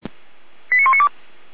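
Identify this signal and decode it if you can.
Five-tone selective call (ZVEI), heard as quick runs of five tones, tones of about 70 ms. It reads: 89192
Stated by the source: ZVEI five-tone selective call received on SDR.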